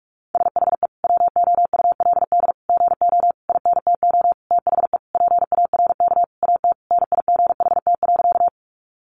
Morse code, heard as a keyed tone, about 44 wpm, 700 Hz. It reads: SHE WOULD GO INTO THE PARK AT NIGHT.
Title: Sentences from Top 1300 Words 45wpm